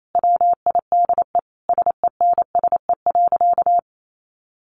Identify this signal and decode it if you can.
Morse code, heard as a keyed tone, about 28 wpm, 700 Hz. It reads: WIDE SENSE.
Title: Sentences from Top 900 Words 28wpm